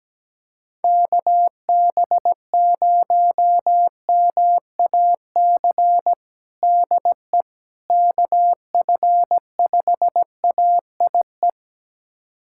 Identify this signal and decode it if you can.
Morse code, heard as a keyed tone, about 17 wpm, 700 Hz. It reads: KB0MAC DE KF5AIE
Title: USA Callsign DE Callsign - 6 Characters - 17wpm